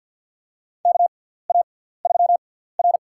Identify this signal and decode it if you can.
Morse code, heard as a keyed tone, about 52 wpm, spaced as for 17 wpm, 700 Hz. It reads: KA3R
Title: USA Call Signs - 6x Character Spacing - 50wpm